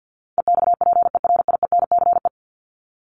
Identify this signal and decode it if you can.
Morse code, heard as a keyed tone, about 50 wpm, 700 Hz. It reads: EXPERIENCE
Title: Top 501-750 Words 50wpm